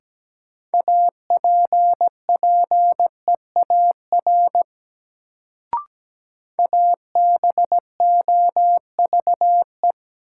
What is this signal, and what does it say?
Morse code, heard as a keyed tone, about 17 wpm, 700 Hz. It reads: APPEAR ABOVE
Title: Sentences from Top 1200 Words - New Word Drill Alt - 17wpm